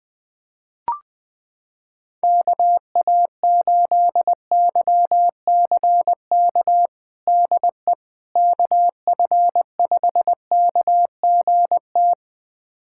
Morse code, heard as a keyed tone, about 20 wpm, 700 Hz. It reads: KA8YCK DE KF5KGT
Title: USA Callsign DE Callsign - 6 Characters - 20wpm